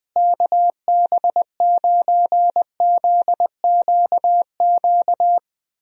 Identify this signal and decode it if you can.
Morse code, heard as a keyed tone, about 20 wpm, 700 Hz. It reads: KB9ZQQ
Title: USA Call Signs - 6 Characters - Repeated 2x - 20wpm